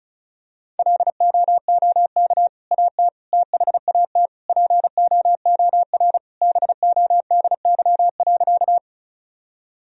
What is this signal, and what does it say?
Morse code, heard as a keyed tone, about 36 wpm, 700 Hz. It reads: LOOK AT THAT POOR BODY.
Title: Sentences from Top 1000 Words - Review Part 2 - 35wpm